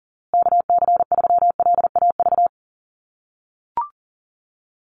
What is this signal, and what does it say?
Morse code, heard as a keyed tone, about 40 wpm, 700 Hz. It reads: KC3LAV
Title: USA Call Signs Repeated 3x 40wpm